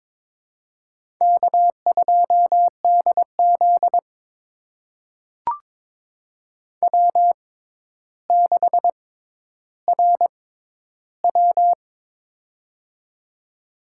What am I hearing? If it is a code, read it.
Morse code, heard as a keyed tone, about 22 wpm, spaced as for 11 wpm, 700 Hz. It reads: K2DZ W6RW
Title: USA Call Signs - 4 Characters - 6x Character Spacing - 22wpm